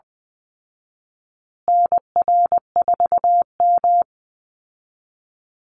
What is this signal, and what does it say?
Morse code, heard as a keyed tone, about 20 wpm, 700 Hz. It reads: NR4M
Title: Call Signs Repeated 2x - 20wpm